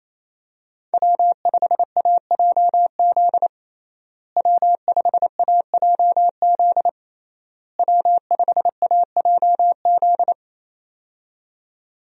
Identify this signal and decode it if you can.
Morse code, heard as a keyed tone, about 28 wpm, 700 Hz. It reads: W5AJZ W5AJZ W5AJZ
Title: USA Call Signs - 5 Characters - Repeated 3x - 28wpm